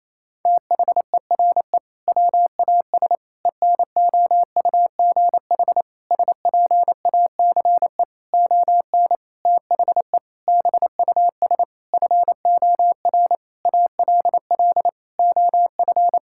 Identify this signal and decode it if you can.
Morse code, heard as a keyed tone, about 27 wpm, 700 Hz. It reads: THERE WAS ENOUGH SPACE ON THE BUS FOR ALL OF
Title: Sentences from Top 1000 Words - Review Part 6 - 28wpm